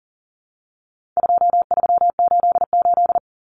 Morse code, heard as a keyed tone, about 40 wpm, 700 Hz. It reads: 2388
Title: Numbers - 4-Digit 40wpm